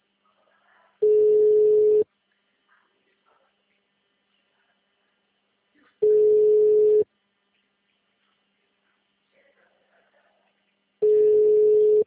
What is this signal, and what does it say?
Ringback tone (Europe, 425 Hz).
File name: out- -2002- - - .46